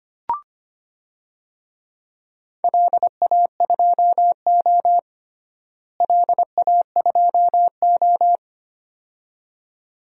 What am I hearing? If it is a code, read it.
Morse code, heard as a keyed tone, about 25 wpm, 700 Hz. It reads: LA2O LA2O
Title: DXCC-100 Call Signs - 4 Characters - Repeated 2x - 25wpm